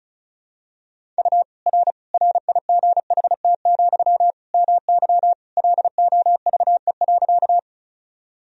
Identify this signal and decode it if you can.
Morse code, heard as a keyed tone, about 34 wpm, 700 Hz. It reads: U R RIGHT, MY LOVE.
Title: Sentences from Top 400 Words and Abbreviations 35wpm